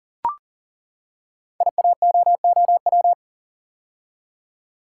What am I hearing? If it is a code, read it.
Morse code, heard as a keyed tone, about 39 wpm, 700 Hz. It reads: IAOOW
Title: Code Groups - 5 Letters - 40wpm